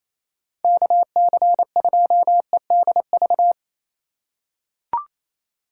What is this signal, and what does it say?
Morse code, heard as a keyed tone, about 28 wpm, 700 Hz. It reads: KC2EDV
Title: USA Call Signs - 6 Characters - 3x Character Spacing - 28wpm